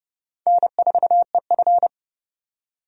Morse code, heard as a keyed tone, about 30 wpm, 700 Hz. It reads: N4EF
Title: USA Call Signs - 4 Characters - 30wpm